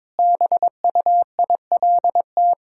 Morse code, heard as a keyed tone, about 22 wpm, 700 Hz. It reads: BUILT
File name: practice / 22wpm / Sentences with 2-Word Phrases - 22wpm